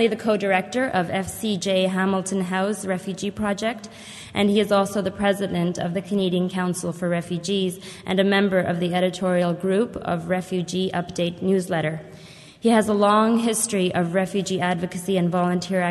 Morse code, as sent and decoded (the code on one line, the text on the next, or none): none